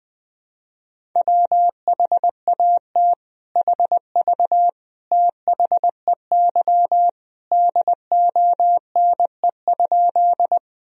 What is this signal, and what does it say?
Morse code, heard as a keyed tone, about 20 wpm, 700 Hz. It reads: WHAT HV THEY DONE?
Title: Sentences from Top 500 Words and Abbreviations 20wpm